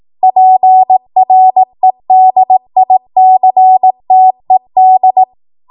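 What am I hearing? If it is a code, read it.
Morse code, heard as a keyed tone, about 18 wpm, 750 Hz. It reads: PREDICTED
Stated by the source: ARRL 18 wpm Morse code practice archive transmission.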